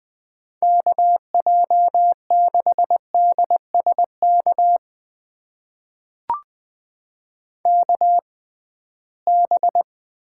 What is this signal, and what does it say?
Morse code, heard as a keyed tone, about 20 wpm, spaced as for 15 wpm, 700 Hz. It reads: KJ6DSK KB
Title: USA Call Signs - 6 Characters - 6x Character Spacing - 20wpm